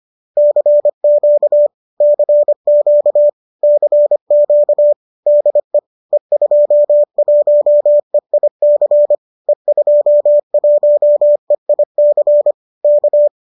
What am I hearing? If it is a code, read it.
Morse code, heard as a keyed tone, about 25 wpm, 570 Hz. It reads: CQ CQ CQ DE E21EIC E21EIC K
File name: qso208 25